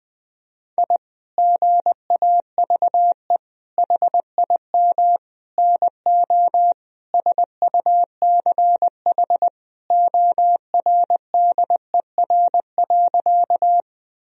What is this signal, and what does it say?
Morse code, heard as a keyed tone, about 21 wpm, 700 Hz. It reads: I GAVE HIM NO SUCH ORDER.
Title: Sentences from Top 1000 Words - Review Part 2 - 20wpm